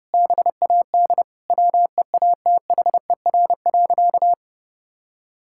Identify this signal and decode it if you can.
Morse code, heard as a keyed tone, about 30 wpm, 700 Hz. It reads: BAD WEATHER.
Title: Sentences from Top 1000 Words - Review Part 6 - 30wpm